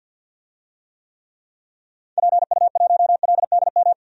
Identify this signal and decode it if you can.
Morse code, heard as a keyed tone, about 47 wpm, 700 Hz. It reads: PU1LDM
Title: DXCC-100 Call Signs - 6 Characters - Repeated 3x - 50wpm